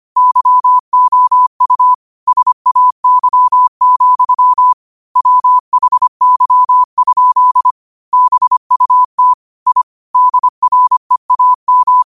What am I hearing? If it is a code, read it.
Morse code, heard as a keyed tone, about 25 wpm, 1000 Hz. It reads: YOU SAY, WHY? BUT I DREAM
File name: qotdfiles / 25wpm